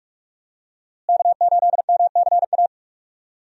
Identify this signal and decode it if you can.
Morse code, heard as a keyed tone, about 45 wpm, 700 Hz. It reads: K8MCA
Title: USA Call Signs - 5 Characters - Repeated 2x - 45wpm